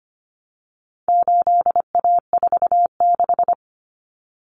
Morse code, heard as a keyed tone, about 25 wpm, 700 Hz. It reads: 8A46
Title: Common Cut Numbers - 4-Digit 25wpm